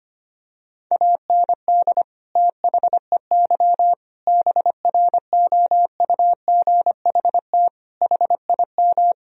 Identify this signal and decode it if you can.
Morse code, heard as a keyed tone, about 25 wpm, 700 Hz. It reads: AND THEY BROUGHT HIM
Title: Sentences from Top 1000 Words - Review Part 2 - 25wpm